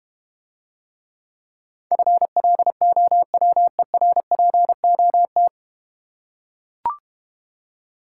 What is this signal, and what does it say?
Morse code, heard as a keyed tone, about 30 wpm, 700 Hz. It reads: FLOWERPOT
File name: Easy Words - Speed-Racing with Hints - 25wpm